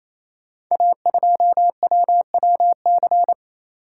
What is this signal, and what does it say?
Morse code, heard as a keyed tone, about 29 wpm, 700 Hz. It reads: A2WWC
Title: Code Groups - 5 Letter:Numbers - 28wpm